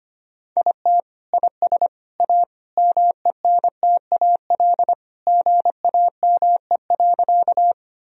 Morse code, heard as a keyed tone, about 25 wpm, 700 Hz. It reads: IT IS A MENTAL GAME.
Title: Sentences from Top 1000 Words - Review Part 4 - 25wpm